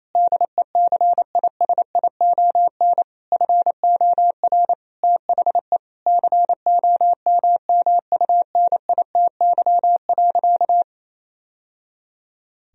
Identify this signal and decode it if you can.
Morse code, heard as a keyed tone, about 28 wpm, 700 Hz. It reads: DECISION FOR THE COMMUNITY.
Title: Sentences from Top 1000 Words - Review Part 4 - 28wpm